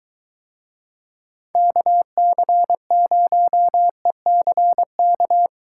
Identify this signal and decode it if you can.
Morse code, heard as a keyed tone, about 23 wpm, 700 Hz. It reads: KC0ECK